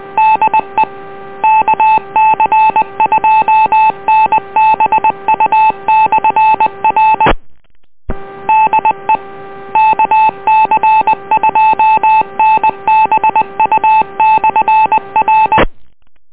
Morse code, heard as a keyed tone, about 20 wpm, 870 Hz. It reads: DE KC2NBU/R DE KC2NBU/R